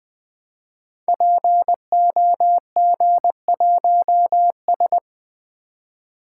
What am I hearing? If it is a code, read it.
Morse code, heard as a keyed tone, about 20 wpm, 700 Hz. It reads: POG1S
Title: Code Groups - 5 Letter:Numbers - 20wpm